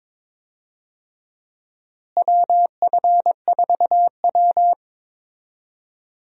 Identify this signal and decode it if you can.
Morse code, heard as a keyed tone, about 22 wpm, 700 Hz. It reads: WF4W
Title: USA Call Signs - 4 Characters - Repeated 2x - 22wpm